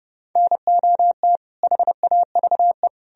Morse code, heard as a keyed tone, about 40 wpm, spaced as for 35 wpm, 700 Hz. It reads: NOT HAVE
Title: Speed-Racing - Sets of 5 Words 30wpm